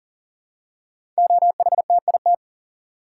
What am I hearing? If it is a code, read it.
Morse code, heard as a keyed tone, about 42 wpm, 690 Hz. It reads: OHTIT
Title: Code Groups - 5 Letters - 40wpm